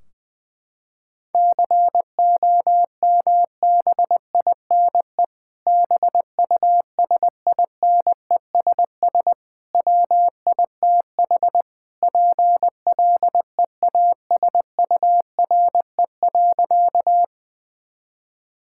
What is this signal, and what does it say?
Morse code, heard as a keyed tone, about 20 wpm, 700 Hz. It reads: COMBINE BUSINESS WITH PLEASURE.
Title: Sentences from Top 1600 Words 20wpm